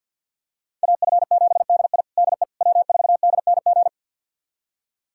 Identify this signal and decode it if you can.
Morse code, heard as a keyed tone, about 46 wpm, 700 Hz. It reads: AF7DI DE W4DNG
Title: USA Callsign DE Callsign - 5 Characters - 50wpm